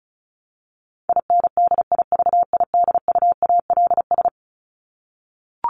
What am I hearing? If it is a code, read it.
Morse code, heard as a keyed tone, about 34 wpm, 700 Hz. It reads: INDIVIDUALS
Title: ES:S Suffix - Top 100 Words with Base Words 35wpm